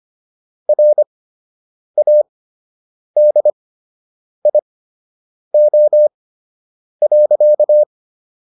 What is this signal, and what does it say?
Morse code, heard as a keyed tone, about 25 wpm, spaced as for 8 wpm, 600 Hz. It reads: RADIO.